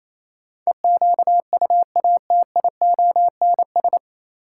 Morse code, heard as a keyed tone, about 28 wpm, 700 Hz. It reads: EQUATIONS
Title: Sets of 5 Words - New Word Drill - 28wpm